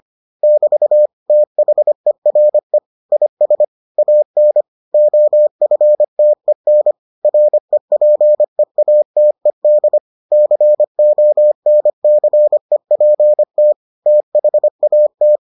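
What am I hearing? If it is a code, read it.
Morse code, heard as a keyed tone, about 25 wpm, 600 Hz. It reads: <BT> THERE IS AN OFTEN REPEATED CONCEPT THAT
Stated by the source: Morse edition of the Foundations of Amateur Radio episodes.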